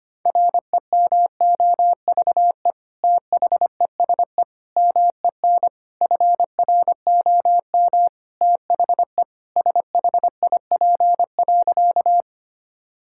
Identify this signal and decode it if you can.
Morse code, heard as a keyed tone, about 25 wpm, 700 Hz. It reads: REMOVE THESE MEN FROM THE SHIP.
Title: Sentences from Top 1000 Words - Review Part 5 - 25wpm